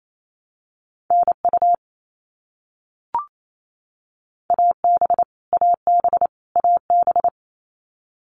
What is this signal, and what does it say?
Morse code, heard as a keyed tone, about 28 wpm, 700 Hz. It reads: NU AB AB AB
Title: Canadian Provinces and Territories - Repeated 3x - 28wpm